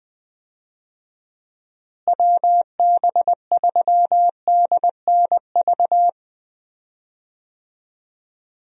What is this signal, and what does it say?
Morse code, heard as a keyed tone, about 21 wpm, 700 Hz. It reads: WB3DNV
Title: USA Call Signs - 6 Characters - Repeated 2x - 20wpm